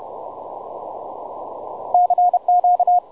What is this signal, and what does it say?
Morse code, heard as a keyed tone, about 30 wpm, 720 Hz, with radio noise. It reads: CQ
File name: juma-trx2 / juma-trx2-20m-cw-sample